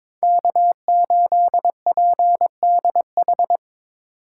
Morse code, heard as a keyed tone, about 22 wpm, 700 Hz. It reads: K8PDH
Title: USA Call Signs - 5 Characters - 4x Character Spacing - 22wpm